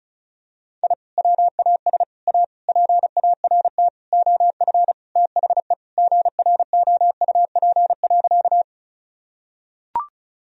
Morse code, heard as a keyed tone, about 36 wpm, 700 Hz. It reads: I WAS A PART OF THE GROUP.
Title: Sentences from Top 1000 Words - Review Part 1 - 35wpm